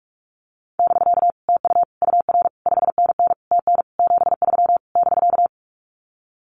Morse code, heard as a keyed tone, about 44 wpm, 700 Hz. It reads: <BK> TU UR 5NN TN 73 <BK>
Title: Common QSO Phrases Encore 45wpm